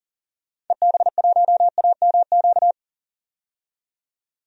Morse code, heard as a keyed tone, about 40 wpm, 700 Hz. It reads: EB1AMQ
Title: DXCC-100 Call Signs - 6 Characters - 40wpm